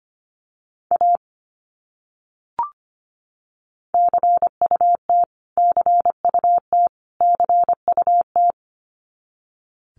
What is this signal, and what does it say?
Morse code, heard as a keyed tone, about 25 wpm, 700 Hz. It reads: A CUT CUT CUT
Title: Top 300 Words Repeated 25wpm